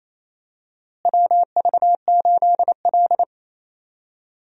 Morse code, heard as a keyed tone, about 28 wpm, 700 Hz. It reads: WV8L
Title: USA Call Signs - 4 Characters - Repeated 2x - 28wpm